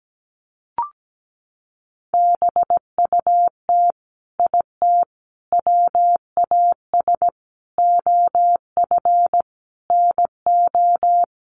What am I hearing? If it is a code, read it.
Morse code, heard as a keyed tone, about 17 wpm, 700 Hz. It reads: BUT IT WAS OF NO
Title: Sentences from Top 100 Words 17wpm